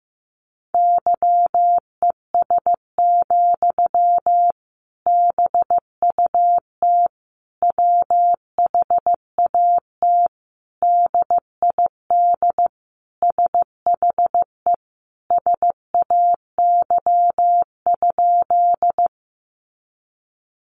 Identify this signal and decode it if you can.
Morse code, heard as a keyed tone, about 14 wpm, 700 Hz. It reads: YES, BUT WHAT DID SHE SAY?
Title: Sentences from Top 500 Words 15wpm